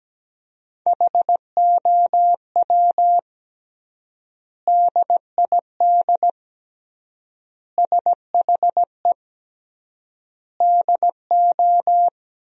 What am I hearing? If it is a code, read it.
Morse code, heard as a keyed tone, about 17 wpm, 700 Hz. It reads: HOW DID SHE DO
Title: 3x Word-Spacing - Sentences from Top 100 Words - 17wpm